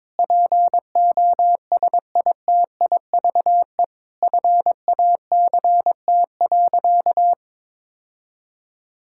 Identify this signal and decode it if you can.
Morse code, heard as a keyed tone, about 22 wpm, 700 Hz. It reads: POSITIVE FACT.
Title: Sentences from Top 1000 Words - Review Part 5 - 22wpm